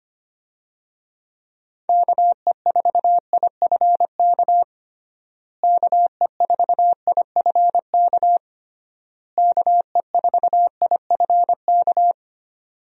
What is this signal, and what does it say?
Morse code, heard as a keyed tone, about 25 wpm, 700 Hz. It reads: KE4IFK KE4IFK KE4IFK
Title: USA Call Signs - 6 Characters - Repeated 3x - 25wpm